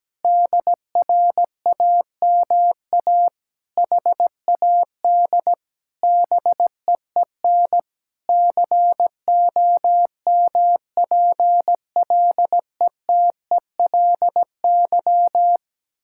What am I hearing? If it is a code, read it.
Morse code, heard as a keyed tone, about 17 wpm, 700 Hz. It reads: DRAMA HAD BEEN COMPLETELY
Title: Sentences from Top 1600 Words 17wpm